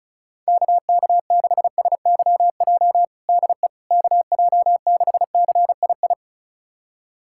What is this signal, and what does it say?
Morse code, heard as a keyed tone, about 34 wpm, 690 Hz. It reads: KK6SYJ DE KJ6CII
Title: USA Callsign DE Callsign - 6 Characters - 35wpm